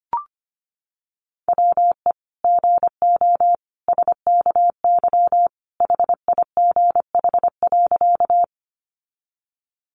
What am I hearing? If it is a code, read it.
Morse code, heard as a keyed tone, about 25 wpm, 700 Hz. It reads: WE GO SKY HIGH.
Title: Sentences from Top 1200 Words 25wpm